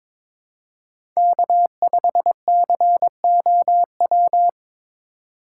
Morse code, heard as a keyed tone, about 22 wpm, 700 Hz. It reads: K5COW
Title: USA Call Signs - 2x Character Spacing - 22wpm